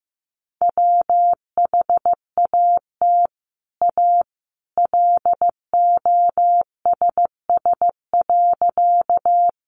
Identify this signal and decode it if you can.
Morse code, heard as a keyed tone, about 15 wpm, 700 Hz. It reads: WHAT A LOSS.
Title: Sentences from Top 1400 Words 15wpm